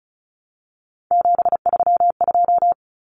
Morse code, heard as a keyed tone, about 34 wpm, 700 Hz. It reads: 732
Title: Common Cut Numbers - 3-Digit 35wpm